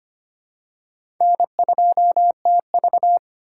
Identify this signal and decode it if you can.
Morse code, heard as a keyed tone, about 25 wpm, 700 Hz. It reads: N2TV
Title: USA Call Signs - 4 Characters - 4x Character Spacing - 25wpm